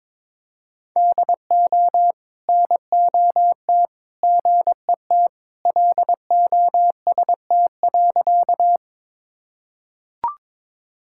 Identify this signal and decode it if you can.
Morse code, heard as a keyed tone, about 22 wpm, 700 Hz. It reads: DO NOT GET LOST.
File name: practice / 22wpm / Sentences with 3-Word Phrases - 22wpm